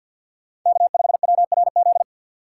Morse code, heard as a keyed tone, about 48 wpm, 700 Hz. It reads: K5WRZ
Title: USA Call Signs - 5 Characters - 6x Character Spacing - 50wpm